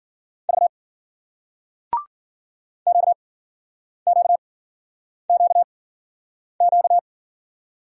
Morse code, heard as a keyed tone, about 56 wpm, spaced as for 9 wpm, 700 Hz. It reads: V QQQQ